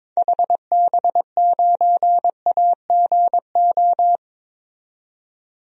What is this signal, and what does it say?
Morse code, heard as a keyed tone, about 22 wpm, 700 Hz. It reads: HB9AGO
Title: DXCC-100 Call Signs - 6 Characters - 22wpm